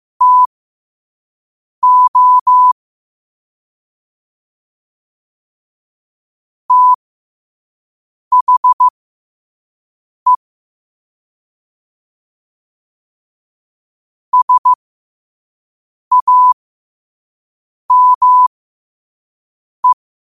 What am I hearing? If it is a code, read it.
Morse code, heard as a keyed tone, about 15 wpm, spaced as for 5 wpm, 1000 Hz. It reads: TO THE SAME
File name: qotdfiles / 5wpm